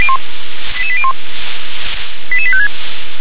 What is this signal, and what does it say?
Five-tone selective call (ZVEI), heard as quick runs of five tones, tones of about 70 ms. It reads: 88801 88856
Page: SDR/ZVEI/88771